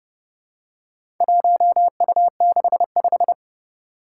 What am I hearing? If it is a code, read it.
Morse code, heard as a keyed tone, about 30 wpm, 700 Hz. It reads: 1U65
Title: Four Challenging Letter-Number Combinations 30wpm